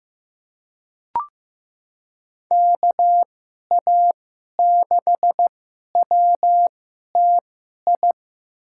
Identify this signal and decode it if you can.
Morse code, heard as a keyed tone, about 15 wpm, spaced as for 11 wpm, 700 Hz. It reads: KA6WTI
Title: USA Call Signs - 2x Character Spacing - 15wpm